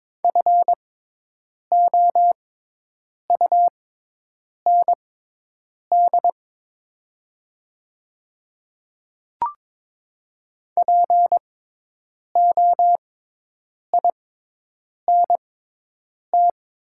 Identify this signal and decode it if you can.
Morse code, heard as a keyed tone, about 22 wpm, spaced as for 8 wpm, 700 Hz. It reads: FOUND POINT
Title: 6x Character Spacing - Top 300 Words - 22wpm